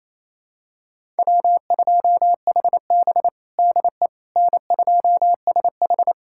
Morse code, heard as a keyed tone, about 28 wpm, 700 Hz. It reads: W2HB DE N2SH
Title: USA Callsign DE Callsign - 4 Characters - 28wpm